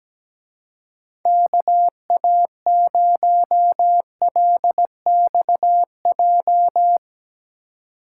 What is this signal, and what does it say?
Morse code, heard as a keyed tone, about 18 wpm, 700 Hz. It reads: KA0LXJ
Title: Speed Racing - USA 2x3 Call Signs 17wpm